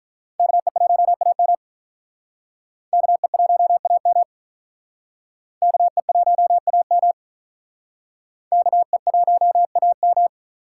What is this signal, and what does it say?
Morse code, heard as a keyed tone, about 46 wpm, 700 Hz. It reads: KE1AM KE1AM KE1AM KE1AM